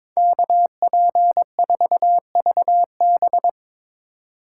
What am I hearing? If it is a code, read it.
Morse code, heard as a keyed tone, about 24 wpm, 700 Hz. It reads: KP4VB
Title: Speed Racing - USA Call Signs 22wpm